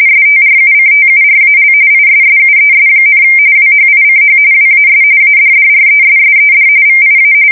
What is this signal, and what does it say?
RTTY, about 45.45 baud. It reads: STRUCTURAL ISSUES, WHICH WAS ADDRESSED VIA
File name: fsk170 / battleships